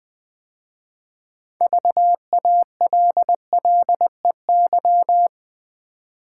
Morse code, heard as a keyed tone, about 21 wpm, 700 Hz. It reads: VALLEY
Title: 6 Letter Words - First 3 Letters Given - 20wpm